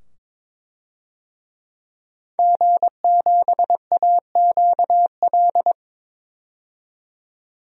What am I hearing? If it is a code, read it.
Morse code, heard as a keyed tone, about 22 wpm, 700 Hz. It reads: G7AQL
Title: DXCC-100 Call Signs - 5 Characters - 22wpm